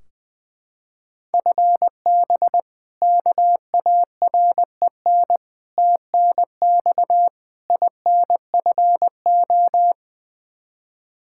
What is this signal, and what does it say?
Morse code, heard as a keyed tone, about 20 wpm, 700 Hz. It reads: FB KAREN TNX INFO
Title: K1USN SST QSO Practice 20wpm